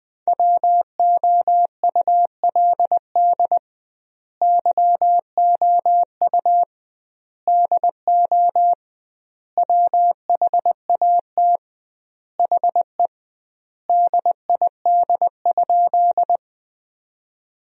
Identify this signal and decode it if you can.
Morse code, heard as a keyed tone, about 20 wpm, 700 Hz. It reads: WOULD YOU DO WHAT HE DID?
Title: 2x Word-Spacing - Sentences from Top 100 Words - 20wpm